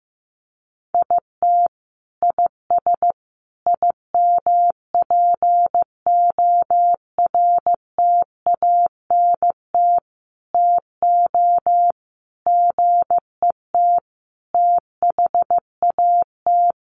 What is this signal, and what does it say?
Morse code, heard as a keyed tone, about 14 wpm, 700 Hz. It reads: IT IS IMPORTANT TO GET THAT
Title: Sentences from Top 1100 Words 15wpm